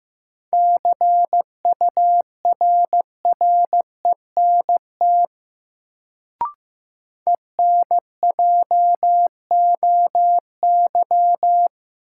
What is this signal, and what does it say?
Morse code, heard as a keyed tone, about 15 wpm, 700 Hz. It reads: CURRENT ENJOY
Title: Sentences from Top 1000 Words - Review Part 4 - New Word Drill - 15wpm